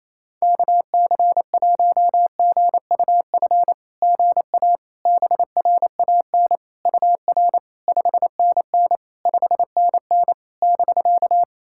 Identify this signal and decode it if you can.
Morse code, heard as a keyed tone, about 27 wpm, 660 Hz. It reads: KC1GUF GA BRAN UR 5NN 5NN <BK>
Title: SOTA QSO Practice 28wpm